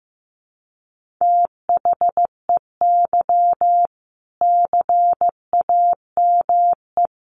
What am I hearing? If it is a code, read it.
Morse code, heard as a keyed tone, about 14 wpm, 700 Hz. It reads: THEY CAME
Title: Sentences from Top 1500 Words 15wpm